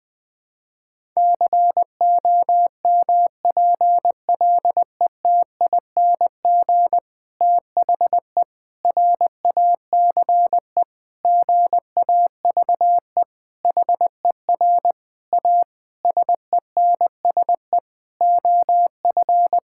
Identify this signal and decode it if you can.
Morse code, heard as a keyed tone, about 21 wpm, 700 Hz. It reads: COMPLETING THE RACE GAVE HER A SENSE OF
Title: Difficult Sentences with Hints - 20wpm